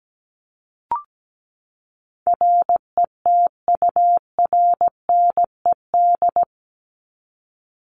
Morse code, heard as a keyed tone, about 17 wpm, 700 Hz. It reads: RETURNED
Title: ED Suffix - Top 100 Words with Base Words 17wpm